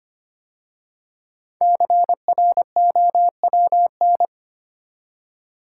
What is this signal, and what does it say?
Morse code, heard as a keyed tone, about 25 wpm, 700 Hz. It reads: CROWN
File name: 5 Letter Words - First 3 Letters Given - 25wpm